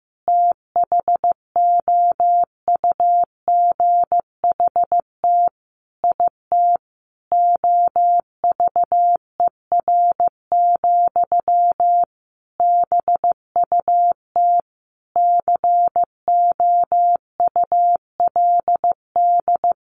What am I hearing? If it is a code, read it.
Morse code, heard as a keyed tone, about 14 wpm, 700 Hz. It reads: THOUGHT IT OVER, BUT COULD
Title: Sentences from Top 1200 Words 15wpm